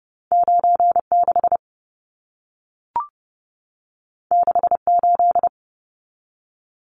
Morse code, heard as a keyed tone, about 30 wpm, 700 Hz. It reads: 96 68
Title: Numbers - 2-Digit 30wpm